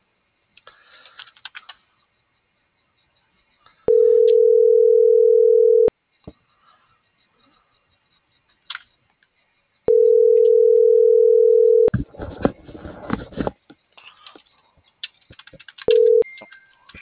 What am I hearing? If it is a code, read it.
Ringback tone (North America), 440 + 480 Hz.